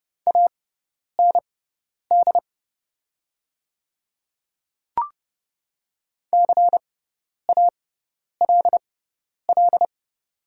Morse code, heard as a keyed tone, about 30 wpm, spaced as for 10 wpm, 700 Hz. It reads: AND CALL
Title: 6x Character Spacing - Top 100 Words - 30wpm